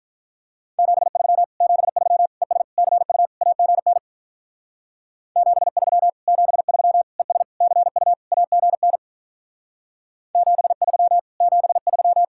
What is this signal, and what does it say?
Morse code, heard as a keyed tone, about 45 wpm, 700 Hz. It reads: 73 73 ES CU AGN 73 73 ES CU AGN 73 73